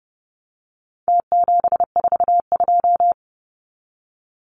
Morse code, heard as a keyed tone, about 30 wpm, 700 Hz. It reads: T742
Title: Numbers - Contest Style - 4-Digit 30wpm